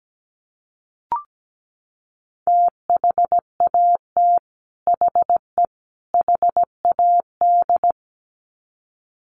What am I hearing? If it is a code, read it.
Morse code, heard as a keyed tone, about 17 wpm, 700 Hz. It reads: THAT HE HAD
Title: Sets of 3 Words 17wpm